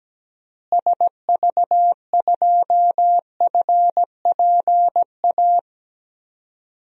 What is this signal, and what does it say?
Morse code, heard as a keyed tone, about 17 wpm, 700 Hz. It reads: SV2FPA
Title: DXCC-100 Call Signs - 6 Characters - Repeated 3x - 17wpm